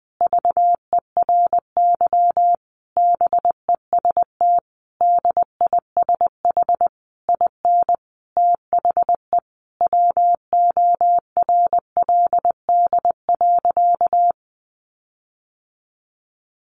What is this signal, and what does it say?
Morse code, heard as a keyed tone, about 20 wpm, 700 Hz. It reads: VERY BEST DISH IN THE WORLD.
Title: Sentences from Top 1300 Words 20wpm